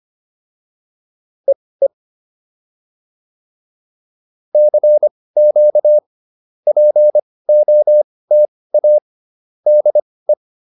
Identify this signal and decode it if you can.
Morse code, heard as a keyed tone, about 25 wpm, spaced as for 18 wpm, 600 Hz. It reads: EE CQ POTA DE